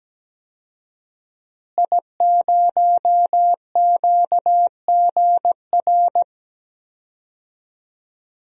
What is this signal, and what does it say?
Morse code, heard as a keyed tone, about 17 wpm, 700 Hz. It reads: I0QGR
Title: DXCC-100 Call Signs - 5 Characters - Repeated 2x - 17wpm